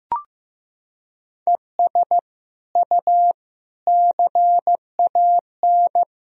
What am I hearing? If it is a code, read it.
Morse code, heard as a keyed tone, about 16 wpm, 700 Hz. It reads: ES U CAN
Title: Sentences from Top 500 Words and Abbreviations 15wpm